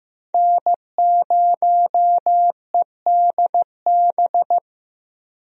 Morse code, heard as a keyed tone, about 15 wpm, 700 Hz. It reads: N0EDB
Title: USA Call Signs - 5 Characters - 15wpm